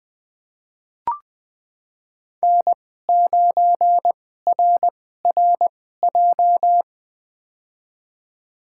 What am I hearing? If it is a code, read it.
Morse code, heard as a keyed tone, about 20 wpm, spaced as for 14 wpm, 700 Hz. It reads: N9RRJ
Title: USA Call Signs - 5 Characters - 2x Character Spacing - 20wpm